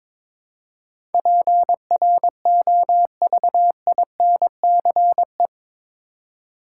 Morse code, heard as a keyed tone, about 22 wpm, 700 Hz. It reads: PROVINCE
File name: practice / 22wpm / Eight Letter Words - 22wpm